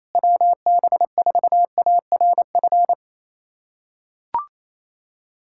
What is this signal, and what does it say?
Morse code, heard as a keyed tone, about 28 wpm, 700 Hz. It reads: WB4ARF
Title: USA Callsign DE Callsign 28wpm